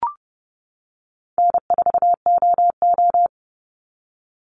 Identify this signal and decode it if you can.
Morse code, heard as a keyed tone, about 30 wpm, 700 Hz. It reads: N4OO
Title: Call Signs - 30wpm